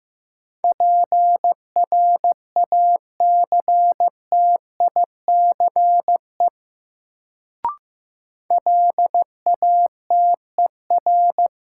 Morse code, heard as a keyed tone, about 15 wpm, 700 Hz. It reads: PRACTICE LATER
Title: Sentences from Top 700 Words - New Word Drill - 15wpm